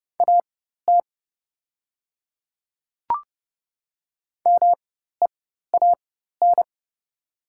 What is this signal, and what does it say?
Morse code, heard as a keyed tone, about 30 wpm, spaced as for 14 wpm, 700 Hz. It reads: AT MEAN
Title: 4x Character Spacing - Top 500 Words - 30wpm